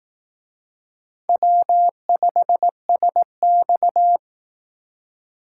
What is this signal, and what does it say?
Morse code, heard as a keyed tone, about 18 wpm, 700 Hz. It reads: W5SX
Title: Speed Racing - USA 1x2 Call Signs 15wpm